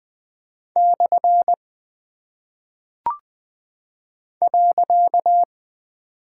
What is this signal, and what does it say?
Morse code, heard as a keyed tone, about 20 wpm, 700 Hz. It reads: / .
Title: Single Punctuation-Prosign - 20wpm